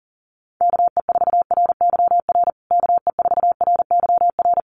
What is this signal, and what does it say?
Morse code, heard as a keyed tone, about 40 wpm, 700 Hz. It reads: KE4RYR KE4RYR
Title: USA Call Signs Repeated 3x 40wpm